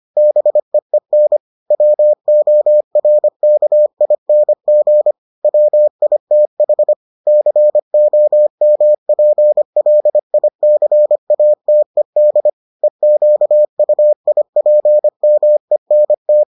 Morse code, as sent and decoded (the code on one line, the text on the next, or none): -... . -. / .-- --- .-. -.- .. -. --. / .-- .. - .... / -.-. --- -- .--. .-.. .. -.-. .- - . -.. / . --.- ..- .. .--. -- . -. -
BEEN WORKING WITH COMPLICATED EQUIPMENT